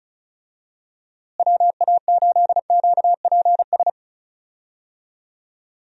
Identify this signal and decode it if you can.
Morse code, heard as a keyed tone, about 34 wpm, 700 Hz. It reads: WA8QPS
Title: USA Call Signs - 6 Characters - Repeated 3x - 35wpm